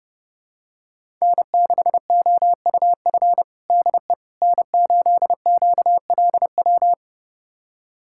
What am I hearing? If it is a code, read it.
Morse code, heard as a keyed tone, about 30 wpm, 700 Hz. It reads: N6OUF DE N8QLW
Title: USA Callsign DE Callsign - 5 Characters - 30wpm